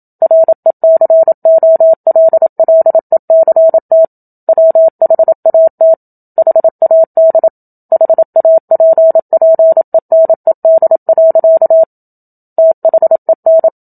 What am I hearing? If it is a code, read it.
Morse code, heard as a keyed tone, about 27 wpm, 650 Hz. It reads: RECOLLECT WHAT HAD HAPPENED. THEN
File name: Book Two / 01-Chapter 1 - 27 WPM